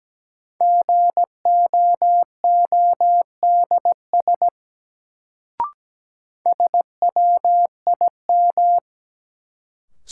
Morse code, heard as a keyed tone, about 17 wpm, 700 Hz. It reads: GOODS SWIM
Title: Sentences from Top 1000 Words - Review Part 6 - New Word Drill - 17wpm